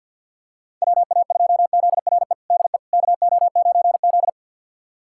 Morse code, heard as a keyed tone, about 50 wpm, 700 Hz. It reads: WA2ZRE DE KO9Z
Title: USA Callsign DE Callsign 50wpm